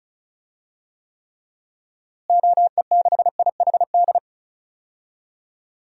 Morse code, heard as a keyed tone, about 36 wpm, 700 Hz. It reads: OE6IHD
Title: DXCC-100 Call Signs - 6 Characters - Repeated 2x - 35wpm